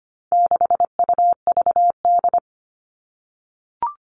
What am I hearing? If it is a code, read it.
Morse code, heard as a keyed tone, about 25 wpm, 700 Hz. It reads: 6UVD
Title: All Cut Numbers - 4-Digit 25wpm